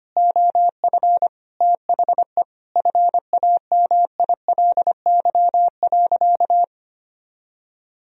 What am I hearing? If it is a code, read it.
Morse code, heard as a keyed tone, about 25 wpm, 700 Hz. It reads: OF THE FAMILY.
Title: Sentences from Top 1000 Words - Review Part 4 - 25wpm